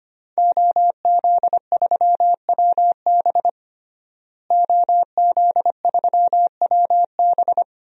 Morse code, heard as a keyed tone, about 25 wpm, 700 Hz. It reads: OZ3WB OZ3WB
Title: DXCC-100 Call Signs - 5 Characters - Repeated 3x - 25wpm